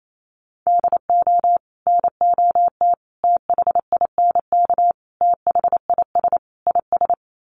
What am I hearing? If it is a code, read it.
Morse code, heard as a keyed tone, about 28 wpm, 700 Hz. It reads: DO NOT THINK THIS IS
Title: Sets of 5 Words Encore2 28wpm